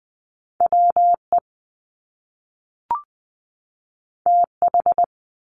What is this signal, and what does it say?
Morse code, heard as a keyed tone, about 20 wpm, 700 Hz. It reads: WE TH
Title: Top Two Letter Combinations in Words 20wpm